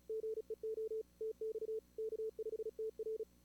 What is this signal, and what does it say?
Morse code, heard as a keyed tone, about 36 wpm, 440 Hz. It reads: GEO TX K5TR
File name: CWSprint / Feb2022 / k5tr-feb-na-sprint-cw-k5ot -2300